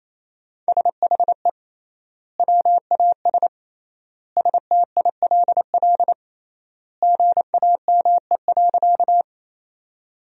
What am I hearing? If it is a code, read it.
Morse code, heard as a keyed tone, about 27 wpm, 700 Hz. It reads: SHE WAS STILL GAME.
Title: 3x Word-Spacing - Sentences from Top 500 Words - 28wpm